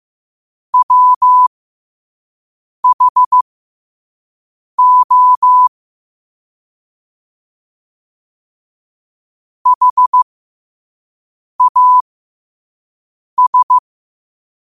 Morse code, heard as a keyed tone, about 15 wpm, spaced as for 5 wpm, 1000 Hz. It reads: WHO HAS
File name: qotdfiles / 5wpm